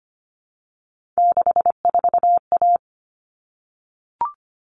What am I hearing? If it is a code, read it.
Morse code, heard as a keyed tone, about 25 wpm, 700 Hz. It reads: 64A
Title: All Cut Numbers - 3-Digit 25wpm